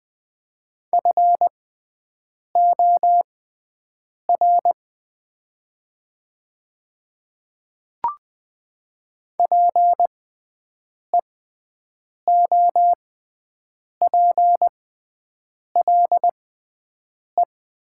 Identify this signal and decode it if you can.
Morse code, heard as a keyed tone, about 20 wpm, spaced as for 7 wpm, 700 Hz. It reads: FOR PEOPLE
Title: 6x Character Spacing - Top 200 Words - 20wpm